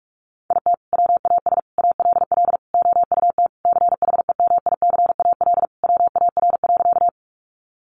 Morse code, heard as a keyed tone, about 44 wpm, 700 Hz. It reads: IT WAS ALL OUT CHEMICAL WAR.
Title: Sentences from Top 1200 Words 45wpm